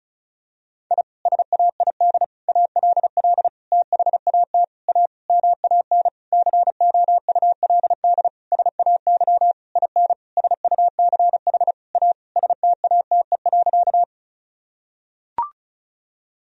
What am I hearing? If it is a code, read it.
Morse code, heard as a keyed tone, about 34 wpm, 700 Hz. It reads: I SAID ALL THAT A MAN COULD SAY IN SUCH A STATE.
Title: Sentences from Top 1000 Words - Review Part 2 - 35wpm